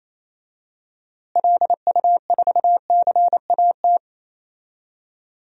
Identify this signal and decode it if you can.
Morse code, heard as a keyed tone, about 27 wpm, 700 Hz. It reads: LU4CAT
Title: DXCC-100 Call Signs - 6 Characters - Repeated 2x - 28wpm